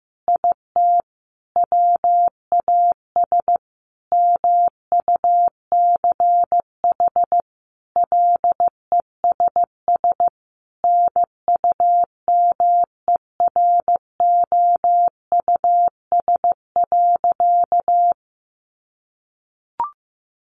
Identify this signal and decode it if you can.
Morse code, heard as a keyed tone, about 15 wpm, 700 Hz. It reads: IT WAS MUCH LESS NUMEROUS.
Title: Sentences from Top 1400 Words 15wpm